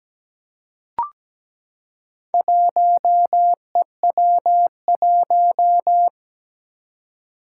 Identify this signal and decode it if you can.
Morse code, heard as a keyed tone, about 17 wpm, 700 Hz. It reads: 1EW1
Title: Four Challenging Letter-Number Combinations 17wpm